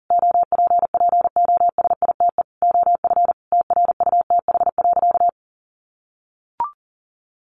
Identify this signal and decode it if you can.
Morse code, heard as a keyed tone, about 40 wpm, 700 Hz. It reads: OPPOSITE OF TRUTH.
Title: Sentences from Top 1300 Words 40wpm